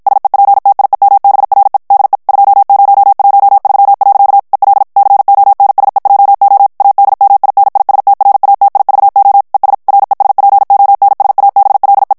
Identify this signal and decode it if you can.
Morse code, heard as a keyed tone, about 54 wpm, 800 Hz. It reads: SEPTIEMBRE DE 2013. EL CONSEJO ADMINISTRATIVO ES RESPONSABLE
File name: ESPAGNOL TEXTE EN CLAIR 1 VITESSE 50 MOTS MINUTES